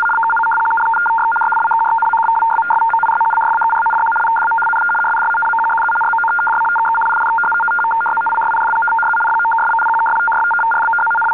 Two-tone teleprinter signal, about 500 Hz shift.
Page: CIS 73 Bd PSEUDO# (phased out)